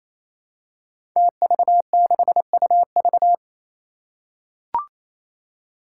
Morse code, heard as a keyed tone, about 27 wpm, 700 Hz. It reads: TV6UV
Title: Five Challenging Letter-Number Combinations 28wpm